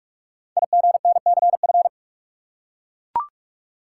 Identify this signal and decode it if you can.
Morse code, heard as a keyed tone, about 44 wpm, 700 Hz. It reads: IGNCF
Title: Code Groups - 5 Letters - 45wpm